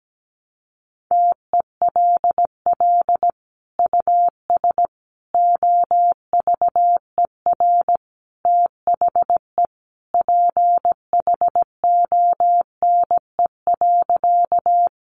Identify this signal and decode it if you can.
Morse code, heard as a keyed tone, about 17 wpm, 700 Hz. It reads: TELL US OVER THE PHONE.
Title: Sentences from Top 1100 Words 17wpm